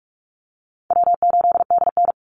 Morse code, heard as a keyed tone, about 44 wpm, 700 Hz. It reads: W8DN
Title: Call Signs - 45wpm